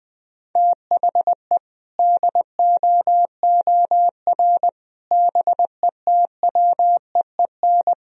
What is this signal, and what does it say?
Morse code, heard as a keyed tone, about 20 wpm, 700 Hz. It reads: THE DOOR BETWEEN
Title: Sentences from Top 1000 Words - Review Part 1 - 20wpm